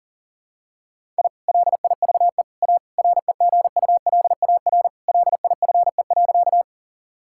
Morse code, heard as a keyed tone, about 37 wpm, 700 Hz. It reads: I LIVE A REGULAR LIFE.
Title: Sentences from Top 1000 Words - Review Part 6 - 40wpm